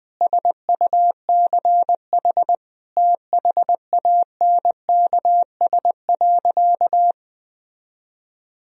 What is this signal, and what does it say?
Morse code, heard as a keyed tone, about 20 wpm, 700 Hz. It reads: SUCH THANKS.
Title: Sentences from Top 1000 Words - Review Part 3 - 20wpm